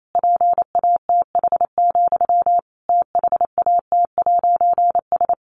Morse code, heard as a keyed tone, about 28 wpm, 700 Hz. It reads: PATH, THAT'S
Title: A Sound Of Thunder 28wpm